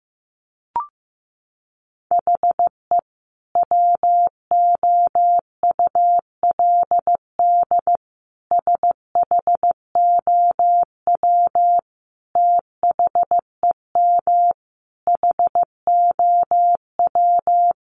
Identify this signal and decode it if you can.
Morse code, heard as a keyed tone, about 15 wpm, 700 Hz. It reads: HE WOULD SHOW THEM HOW
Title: Sentences from Top 1300 Words 15wpm